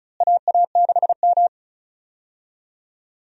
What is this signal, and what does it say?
Morse code, heard as a keyed tone, about 36 wpm, 700 Hz. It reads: AA6M
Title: USA Call Signs - 4 Characters - Repeated 3x - 35wpm